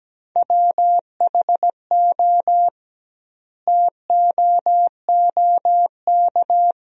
Morse code, heard as a keyed tone, about 17 wpm, 700 Hz. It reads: WHO TOOK
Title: 2x Word-Spacing - Sentences from Top 300 Words - 17wpm